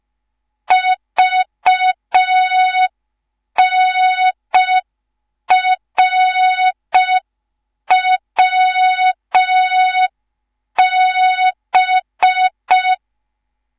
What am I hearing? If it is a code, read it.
Morse code, heard as a keyed tone, about 5 wpm, 3700 Hz. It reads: VNRWB
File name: Morze-HA2MN / hang / 10GWZ HA2MN